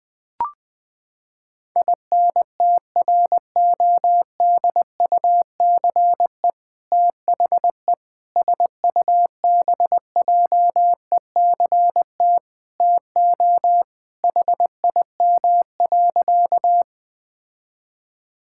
Morse code, heard as a keyed tone, about 21 wpm, 700 Hz. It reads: INTRODUCE THE SUBJECT TO HIM.
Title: Sentences from Top 1600 Words 20wpm